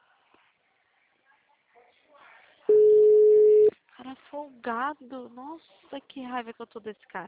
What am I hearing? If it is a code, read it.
Ringback tone (Europe, 425 Hz), with speech.
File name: out- -2013- - - .25122